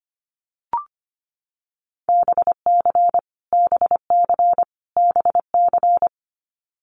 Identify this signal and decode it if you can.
Morse code, heard as a keyed tone, about 25 wpm, 700 Hz. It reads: BC BC BC
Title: Canadian Provinces and Territories - Repeated 3x - 25wpm